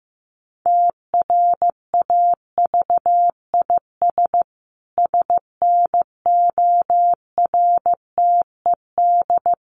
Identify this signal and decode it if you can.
Morse code, heard as a keyed tone, about 15 wpm, 700 Hz. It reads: TRAVIS SNORTED
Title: A Sound Of Thunder 15wpm